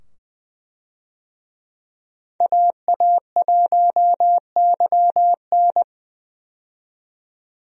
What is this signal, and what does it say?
Morse code, heard as a keyed tone, about 20 wpm, 700 Hz. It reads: AA1YN
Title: USA Call Signs - 5 Characters - 20wpm